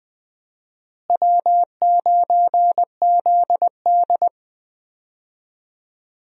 Morse code, heard as a keyed tone, about 20 wpm, 700 Hz. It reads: W9ZD
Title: USA Call Signs - 4 Characters - Repeated 2x - 20wpm